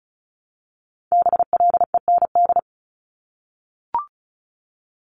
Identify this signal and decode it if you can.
Morse code, heard as a keyed tone, about 36 wpm, 700 Hz. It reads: BLEND
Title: Multi-letter Phonemes 35wpm